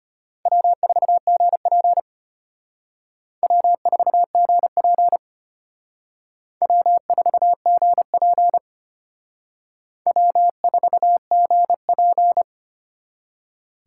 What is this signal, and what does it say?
Morse code, heard as a keyed tone, about 32 wpm, 700 Hz. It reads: W4GP W4GP W4GP W4GP